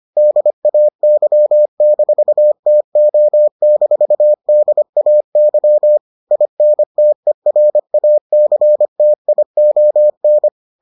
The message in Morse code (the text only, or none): DAY-TO-DAY INTERACTION